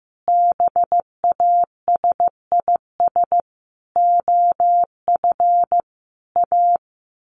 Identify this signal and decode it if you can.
Morse code, heard as a keyed tone, about 15 wpm, 700 Hz. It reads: BASIS OF A
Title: Sets of 5 Words Encore 15wpm